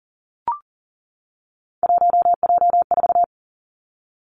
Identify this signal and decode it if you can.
Morse code, heard as a keyed tone, about 40 wpm, 700 Hz. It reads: 1J4
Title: Three Challenging Letter-Number Combinations 40wpm